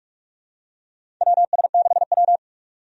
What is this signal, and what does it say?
Morse code, heard as a keyed tone, about 46 wpm, 700 Hz. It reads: WS6W
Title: USA Call Signs - 4 Characters - Repeated 3x - 45wpm